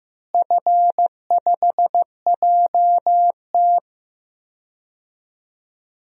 Morse code, heard as a keyed tone, about 16 wpm, 700 Hz. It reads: F5JT
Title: DXCC-100 Call Signs - 4 Characters - Repeated 3x - 15wpm